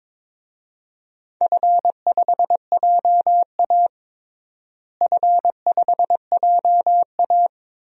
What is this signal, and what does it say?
Morse code, heard as a keyed tone, about 22 wpm, 700 Hz. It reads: F5JA F5JA
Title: DXCC-100 Call Signs - 4 Characters - Repeated 3x - 22wpm